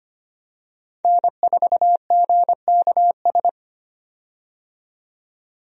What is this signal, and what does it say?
Morse code, heard as a keyed tone, about 25 wpm, 700 Hz. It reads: N4GKS
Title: USA Call Signs - 5 Characters - 25wpm